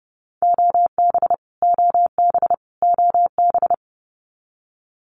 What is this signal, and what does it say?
Morse code, heard as a keyed tone, about 30 wpm, 700 Hz. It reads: OB OB OB
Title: Abbreviations Repeated 3x - 30wpm